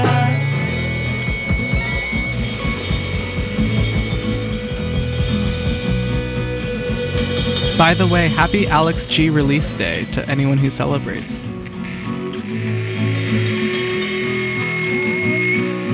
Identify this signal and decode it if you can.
Dial-up modem handshake, with speech.